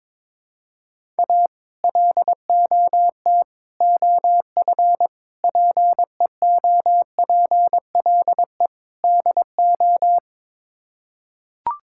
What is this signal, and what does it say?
Morse code, heard as a keyed tone, about 22 wpm, 700 Hz. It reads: A LOT OF PEOPLE DO
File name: practice / 22wpm / Build A Phrase - 5-Word Phrases - 22wpm